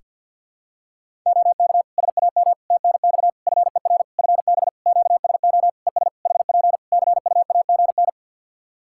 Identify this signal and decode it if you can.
Morse code, heard as a keyed tone, about 51 wpm, 700 Hz. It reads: OK SAM TNX FER FB QSO ES HP CUAGN
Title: Practice QSO - Exchange 4 - Ending - 50wpm